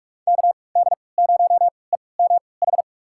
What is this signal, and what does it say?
Morse code, heard as a keyed tone, about 46 wpm, spaced as for 25 wpm, 700 Hz. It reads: KD0EMH
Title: USA Call Signs - 6 Characters - 3x Character Spacing - 45wpm